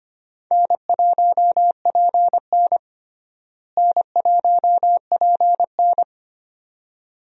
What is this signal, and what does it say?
Morse code, heard as a keyed tone, about 25 wpm, 700 Hz. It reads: N1PN N1PN
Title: USA Call Signs - 4 Characters - Repeated 2x - 25wpm